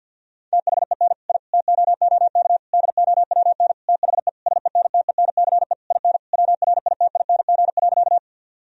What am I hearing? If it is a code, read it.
Morse code, heard as a keyed tone, about 52 wpm, 700 Hz. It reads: THEN I TOOK DOWN THE SENTENCE IN WRITING.
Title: Sentences from Top 1000 Words - Review Part 1 - 50wpm